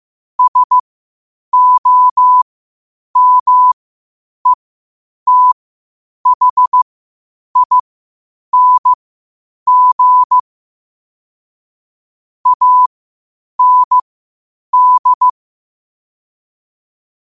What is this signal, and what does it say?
Morse code, heard as a keyed tone, about 15 wpm, spaced as for 8 wpm, 1000 Hz. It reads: SOMETHING AND
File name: qotdfiles / 8wpm